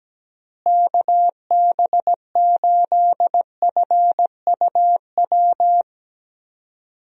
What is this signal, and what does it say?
Morse code, heard as a keyed tone, about 17 wpm, 700 Hz. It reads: KB8FUW
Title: USA Call Signs - 6 Characters - Repeated 3x - 17wpm